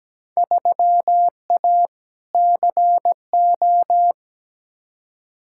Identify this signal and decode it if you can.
Morse code, heard as a keyed tone, about 17 wpm, 700 Hz. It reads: 3A CO
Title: Field Day Exchanges Encore - 17wpm